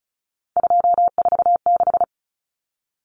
Morse code, heard as a keyed tone, about 34 wpm, 700 Hz. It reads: 246
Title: Numbers - 3-Digit 35wpm